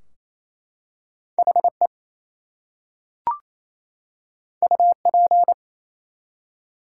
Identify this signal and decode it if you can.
Morse code, heard as a keyed tone, about 27 wpm, 700 Hz. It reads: HE UP
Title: Two Letter Words - 28wpm